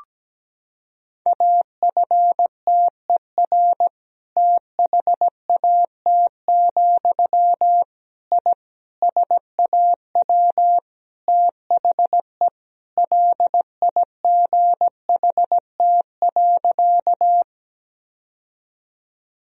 Morse code, heard as a keyed tone, about 17 wpm, 700 Hz. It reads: AFTER THAT, I SAW THE LIGHT.
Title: Sentences from Top 1000 Words - Review Part 1 - 17wpm